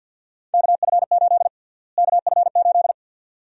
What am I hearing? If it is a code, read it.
Morse code, heard as a keyed tone, about 51 wpm, 700 Hz. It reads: KF8 KF8
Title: USA Call Sign Prefixes - Repeated 2x - 50wpm